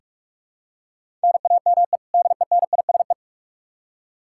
Morse code, heard as a keyed tone, about 46 wpm, 700 Hz. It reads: NAME DENISE
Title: Practice QSO - Exchange 1 - Beginning - 45wpm